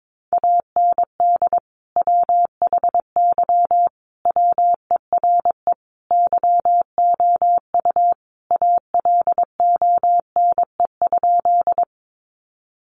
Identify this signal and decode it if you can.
Morse code, heard as a keyed tone, about 22 wpm, 700 Hz. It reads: AND WHY WERE YOU ALONE?
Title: Sentences from Top 1000 Words 22wpm